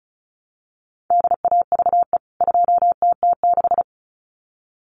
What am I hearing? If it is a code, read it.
Morse code, heard as a keyed tone, about 34 wpm, 700 Hz. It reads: DAVE 2TT6
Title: CWT Exchange 35wpm